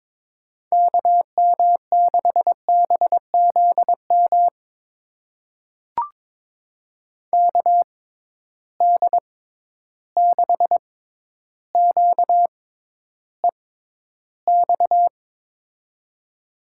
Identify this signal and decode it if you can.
Morse code, heard as a keyed tone, about 22 wpm, spaced as for 11 wpm, 700 Hz. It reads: KM6BZM KD6QEX
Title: USA Call Signs - 6 Characters - 6x Character Spacing - 22wpm